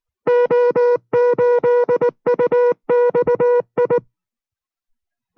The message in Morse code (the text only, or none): O8UXI